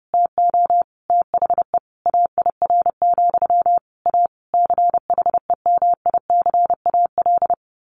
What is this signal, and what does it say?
Morse code, heard as a keyed tone, about 30 wpm, 700 Hz. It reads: TO THE AIR, A CHEMICAL
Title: A Sound Of Thunder 30wpm